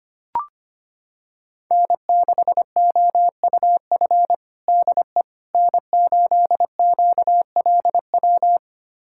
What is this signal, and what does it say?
Morse code, heard as a keyed tone, about 25 wpm, 700 Hz. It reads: N6OUF DE N8QLW
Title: USA Callsign DE Callsign - 5 Characters - 25wpm